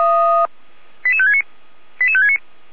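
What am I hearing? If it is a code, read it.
Five-tone selective call (ZVEI), heard as quick runs of five tones, tones of about 80 ms. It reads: 88569 88569
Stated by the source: ZVEI five-tone selective call 88491, received on an SDR.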